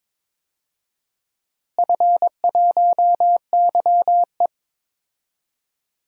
Morse code, heard as a keyed tone, about 22 wpm, 700 Hz. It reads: F1YE
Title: DXCC-100 Call Signs - 4 Characters - Repeated 3x - 22wpm